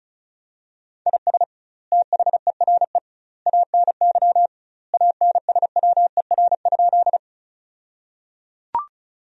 Morse code, heard as a keyed tone, about 36 wpm, 700 Hz. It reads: IS THERE ANY ANSWER?
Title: 2x Word-Spacing - Sentences from Top 300 Words - 35wpm